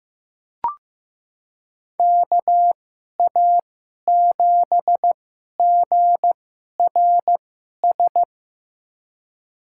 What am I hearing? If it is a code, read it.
Morse code, heard as a keyed tone, about 15 wpm, spaced as for 11 wpm, 700 Hz. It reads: KA7GRS
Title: USA Call Signs - 2x Character Spacing - 15wpm